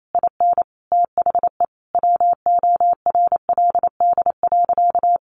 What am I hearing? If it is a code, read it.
Morse code, heard as a keyed tone, about 28 wpm, 700 Hz. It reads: IN THE WORLD.
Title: Sentences from Top 200 Words 28wpm